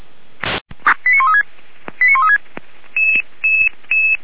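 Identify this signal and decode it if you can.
Five-tone selective call (ZVEI), heard as quick runs of five tones, tones of about 70 ms. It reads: 89136 89136
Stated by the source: ZVEI five-tone selective call to address 89135, received on an SDR by Feuerwehr Orlamünde, a fire brigade.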